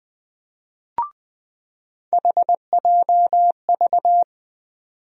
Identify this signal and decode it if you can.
Morse code, heard as a keyed tone, about 20 wpm, 700 Hz. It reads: HJV
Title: USA Call Sign Postfixes - 3 Letters 20wpm